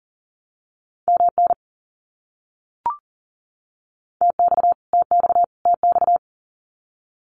Morse code, heard as a keyed tone, about 40 wpm, 700 Hz. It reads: MN TX TX TX
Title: US States Repeated 3x 40wpm